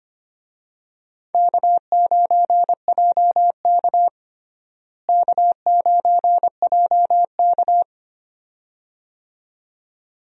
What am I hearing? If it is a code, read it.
Morse code, heard as a keyed tone, about 25 wpm, 700 Hz. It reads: K9JK K9JK
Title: USA Call Signs - 4 Characters - Repeated 2x - 25wpm